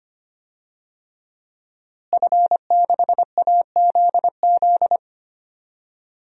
Morse code, heard as a keyed tone, about 25 wpm, 700 Hz. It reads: F6AZZ
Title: DXCC-100 Call Signs - 5 Characters - Repeated 3x - 25wpm